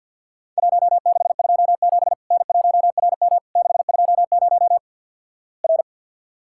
Morse code, heard as a keyed tone, about 49 wpm, 700 Hz. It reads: 1627 N1RM 620 R
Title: Worked All Europe DX Contest - QTC Exchange - 50wpm